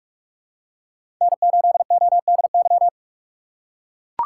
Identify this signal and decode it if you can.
Morse code, heard as a keyed tone, about 44 wpm, 690 Hz. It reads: N8ODY
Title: USA Call Signs - 5 Characters - 4x Character Spacing - 45wpm